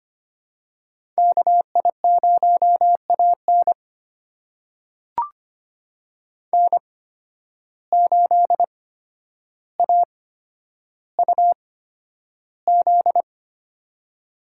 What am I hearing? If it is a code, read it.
Morse code, heard as a keyed tone, about 25 wpm, spaced as for 11 wpm, 700 Hz. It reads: KI0AN N8AUZ
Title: USA Call Signs - 8x Character Spacing - 25wpm